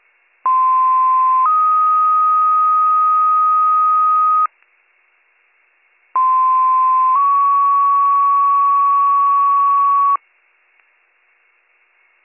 Two-tone page, heard as a one-second tone, then a three-second tone, about 1040 then 1250 Hz, about 1040 then 1120 Hz. It is heard from 0.4 to 10.2 s.